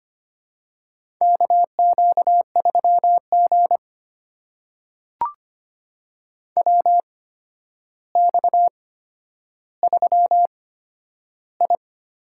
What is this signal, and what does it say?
Morse code, heard as a keyed tone, about 25 wpm, spaced as for 11 wpm, 700 Hz. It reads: KQ3G WX3I
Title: USA Call Signs - 4 Characters - 8x Character Spacing - 25wpm